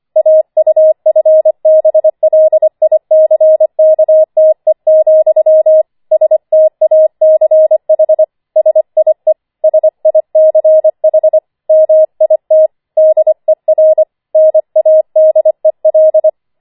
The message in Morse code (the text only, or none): AUFBLICKTE, STACH SIE SICH MIT DER NADEL